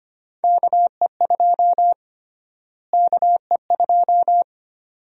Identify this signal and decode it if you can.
Morse code, heard as a keyed tone, about 25 wpm, 700 Hz. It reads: KE2 KE2
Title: USA Call Sign Prefixes - Repeated 2x - 25wpm